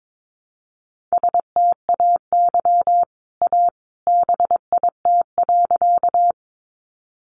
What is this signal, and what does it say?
Morse code, heard as a keyed tone, about 22 wpm, 700 Hz. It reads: STAY A BIT.
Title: Sentences from Top 900 Words 22wpm